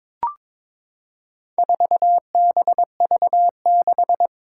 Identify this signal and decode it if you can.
Morse code, heard as a keyed tone, about 22 wpm, 700 Hz. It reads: 4BV6
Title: Four Challenging Letter-Number Combinations 22wpm